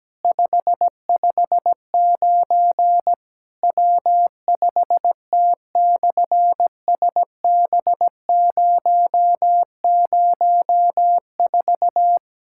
Morse code, heard as a keyed tone, about 17 wpm, 700 Hz. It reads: 559 W5T/SB004
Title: SOTA QSO Practice 17wpm